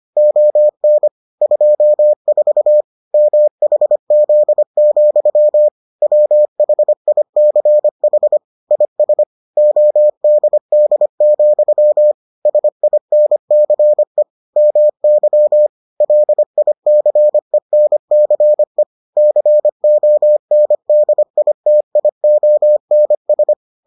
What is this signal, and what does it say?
Morse code, heard as a keyed tone, about 25 wpm, 600 Hz. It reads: ON 24 MHZ, WHICH IS ODD, SINCE MY LICENCE CONDITIONS